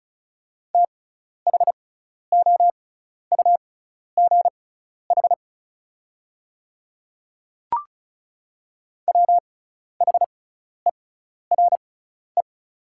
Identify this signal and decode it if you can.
Morse code, heard as a keyed tone, about 36 wpm, spaced as for 12 wpm, 690 Hz. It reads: THOUGH WHERE
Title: 6x Character Spacing - Top 500 Words - 35wpm